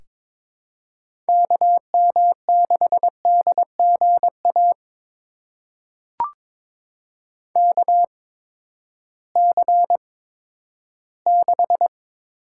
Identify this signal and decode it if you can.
Morse code, heard as a keyed tone, about 22 wpm, spaced as for 12 wpm, 700 Hz. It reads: KM6DGA KC6
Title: USA Call Signs - 6 Characters - 8x Character Spacing - 22wpm